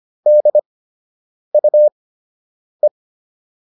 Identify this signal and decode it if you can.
Morse code, heard as a keyed tone, about 25 wpm, spaced as for 8 wpm, 600 Hz. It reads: DUE